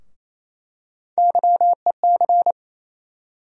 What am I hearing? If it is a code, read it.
Morse code, heard as a keyed tone, about 28 wpm, 700 Hz. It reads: YEC
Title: USA Call Sign Postfixes - 3 Letters 28wpm